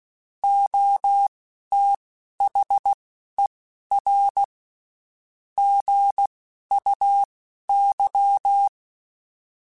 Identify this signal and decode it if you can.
Morse code, heard as a keyed tone, about 16 wpm, spaced as for 11 wpm, 800 Hz. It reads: OTHER GUY
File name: Spaced / Chuckles Spaced-mp3s / HumanErr sp @15